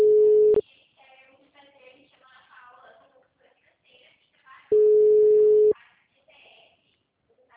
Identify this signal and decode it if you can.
Ringback tone (Europe, 425 Hz), with speech.